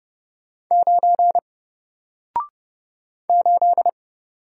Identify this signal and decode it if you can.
Morse code, heard as a keyed tone, about 30 wpm, 700 Hz. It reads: 9 8
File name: Single-Letter-Number-30wpm